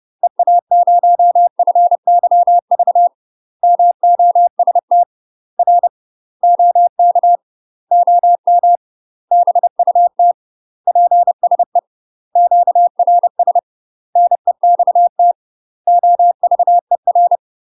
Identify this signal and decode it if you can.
Morse code, heard as a keyed tone, about 30 wpm, 700 Hz. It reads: EA0FYV MOST R OK OM BUT PSE QRS NEXT OVER
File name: UK MASTER QSOs TRAINING / 30WPM / UK MASTER QSOs 0003